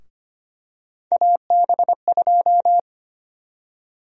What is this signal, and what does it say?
Morse code, heard as a keyed tone, about 25 wpm, 700 Hz. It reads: AB2
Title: USA Call Sign Prefixes - Repeated 2x - 25wpm